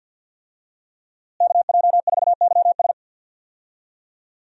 Morse code, heard as a keyed tone, about 52 wpm, 700 Hz. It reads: KJ4YS
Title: USA Call Signs - 5 Characters - Repeated 3x - 50wpm